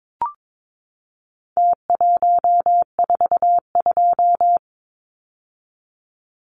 Morse code, heard as a keyed tone, about 22 wpm, 700 Hz. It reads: T142
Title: Numbers - Contest Style - 4-Digit 22wpm